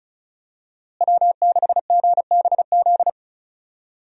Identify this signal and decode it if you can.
Morse code, heard as a keyed tone, about 36 wpm, 700 Hz. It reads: W6GBZ
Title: USA Call Signs - 5 Characters - Repeated 2x - 35wpm